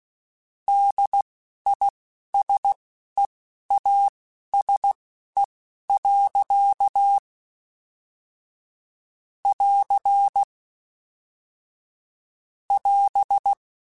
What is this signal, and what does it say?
Morse code, heard as a keyed tone, about 16 wpm, spaced as for 11 wpm, 800 Hz. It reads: DISEASE. <AR> <AS>